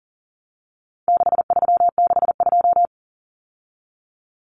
Morse code, heard as a keyed tone, about 40 wpm, 700 Hz. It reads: 6362
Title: Numbers - 4-Digit 40wpm